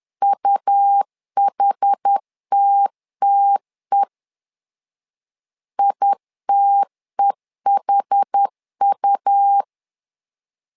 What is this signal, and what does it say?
Morse code, heard as a keyed tone, about 10 wpm, 800 Hz. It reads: UHTTE ITEHU
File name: LEZIONI / 2 UHDN / 2 UH 10wpm 1714